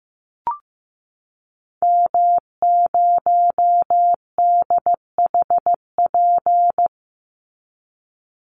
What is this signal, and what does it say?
Morse code, heard as a keyed tone, about 15 wpm, 700 Hz. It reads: M0DHP
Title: Call Signs Repeated 2x - 15wpm